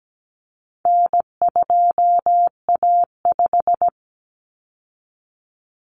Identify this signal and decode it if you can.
Morse code, heard as a keyed tone, about 17 wpm, 700 Hz. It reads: N2A5
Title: Common Cut Numbers - 4-Digit 17wpm